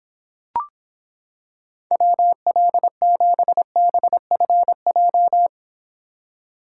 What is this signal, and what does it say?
Morse code, heard as a keyed tone, about 26 wpm, 700 Hz. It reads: WL7BFJ